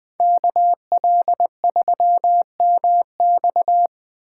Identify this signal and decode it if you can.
Morse code, heard as a keyed tone, about 20 wpm, 700 Hz. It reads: KL3MX